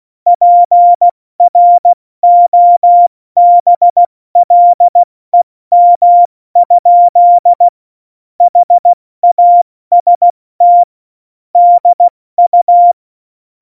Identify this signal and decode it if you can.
Morse code, heard as a keyed tone, about 16 wpm, 700 Hz. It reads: PROBLEM? HAST DU
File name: Morsekurs / Witze / Witz 27. November 2018, 70 F80